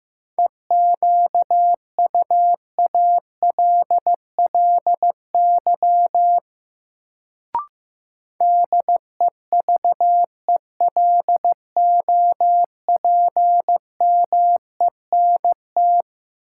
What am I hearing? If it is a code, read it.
Morse code, heard as a keyed tone, about 15 wpm, 700 Hz. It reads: EQUALLY DEVELOPMENT
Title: Sentences from Top 1400 Words - New Word Drill Alt - 15wpm